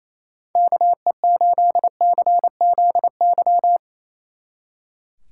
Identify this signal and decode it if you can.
Morse code, heard as a keyed tone, about 34 wpm, 700 Hz. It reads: KE8CZY
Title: Speed Racing - USA 2x3 Call Signs 28wpm